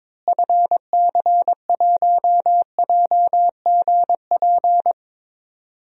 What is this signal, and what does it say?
Morse code, heard as a keyed tone, about 21 wpm, 700 Hz. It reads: FC1JGP
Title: DXCC-100 Call Signs - 6 Characters - Repeated 3x - 22wpm